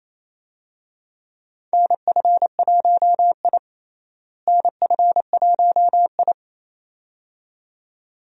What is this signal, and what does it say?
Morse code, heard as a keyed tone, about 28 wpm, 700 Hz. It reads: NF1I NF1I
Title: USA Call Signs - 4 Characters - Repeated 2x - 28wpm